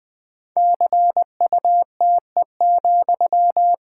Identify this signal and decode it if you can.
Morse code, heard as a keyed tone, about 20 wpm, 700 Hz. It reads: CUTE,
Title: Pangram Challenge v2 - 20wpm